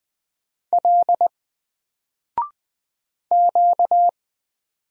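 Morse code, heard as a keyed tone, about 20 wpm, 700 Hz. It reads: L Q
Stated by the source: Single letter and number Morse drill at 20 wpm, from N4LCD.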